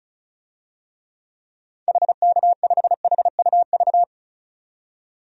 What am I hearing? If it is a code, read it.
Morse code, heard as a keyed tone, about 34 wpm, 700 Hz. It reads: HK5HUV
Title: DXCC-100 Call Signs - 6 Characters - 35wpm